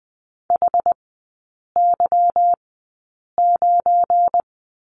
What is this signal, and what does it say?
Morse code, heard as a keyed tone, about 20 wpm, spaced as for 8 wpm, 700 Hz. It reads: HY9
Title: 191 Lesson 31e - Letter v - Three Characters 20wpm